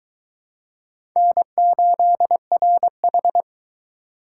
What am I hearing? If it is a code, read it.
Morse code, heard as a keyed tone, about 22 wpm, 700 Hz. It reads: N8RH